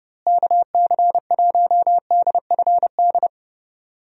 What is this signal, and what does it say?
Morse code, heard as a keyed tone, about 30 wpm, 700 Hz. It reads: KC1DFD
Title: USA Call Signs - 6 Characters - Repeated 2x - 30wpm